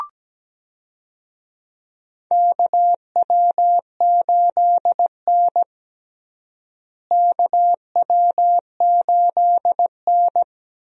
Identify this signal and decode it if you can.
Morse code, heard as a keyed tone, about 17 wpm, 700 Hz. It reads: KW8N KW8N
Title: USA Call Signs - 4 Characters - Repeated 2x - 17wpm